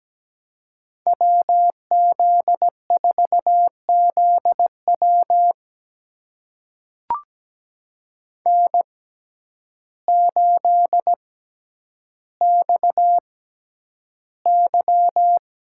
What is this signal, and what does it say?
Morse code, heard as a keyed tone, about 17 wpm, spaced as for 9 wpm, 700 Hz. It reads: WZ4ZW N8XY
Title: USA Call Signs - 6x Character Spacing - 17wpm